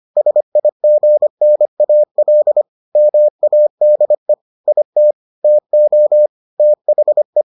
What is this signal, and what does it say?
Morse code, heard as a keyed tone, about 25 wpm, 600 Hz. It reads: SIGNAL MADE IT TO THE